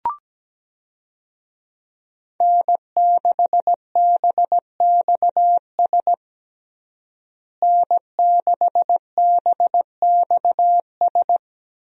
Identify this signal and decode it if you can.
Morse code, heard as a keyed tone, about 17 wpm, 700 Hz. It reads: N6BXS N6BXS
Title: USA Call Signs - 5 Characters - Repeated 3x - 17wpm